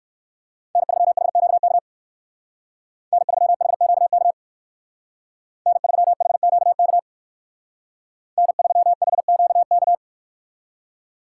Morse code, heard as a keyed tone, about 58 wpm, 700 Hz. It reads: N3HQK N3HQK N3HQK N3HQK